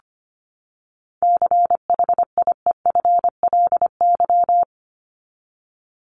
Morse code, heard as a keyed tone, about 24 wpm, 700 Hz. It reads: CHIEFLY
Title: LY Suffix - Top 100 Words with Base Words 25wpm